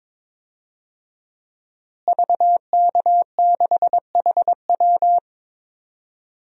Morse code, heard as a keyed tone, about 22 wpm, 700 Hz. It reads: VK6HW
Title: DXCC-100 Call Signs - 5 Characters - Repeated 2x - 22wpm